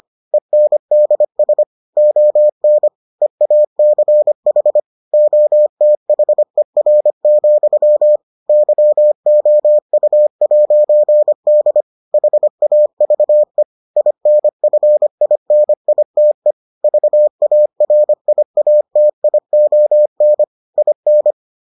Morse code, heard as a keyed tone, about 25 wpm, 600 Hz. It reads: ENDS ON EACH OTHER, YOU'D HAVE INFINITE VARIATION IN